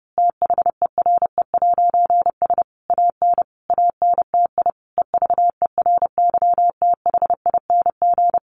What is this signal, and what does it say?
Morse code, heard as a keyed tone, about 30 wpm, 700 Hz. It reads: THERE'S AN ANTI EVERYTHING
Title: A Sound Of Thunder 30wpm